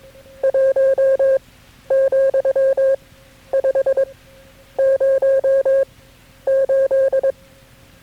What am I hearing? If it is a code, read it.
Morse code, heard as a keyed tone, about 22 wpm, spaced as for 11 wpm, 540 Hz. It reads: 1,508
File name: KB6MT-tape3-side1